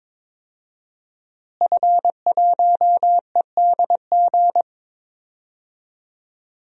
Morse code, heard as a keyed tone, about 22 wpm, 700 Hz. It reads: F1EDG
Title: DXCC-100 Call Signs - 5 Characters - Repeated 2x - 22wpm